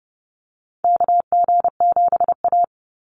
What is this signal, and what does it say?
Morse code, heard as a keyed tone, about 30 wpm, 700 Hz. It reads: KG7A
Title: Call Signs - 30wpm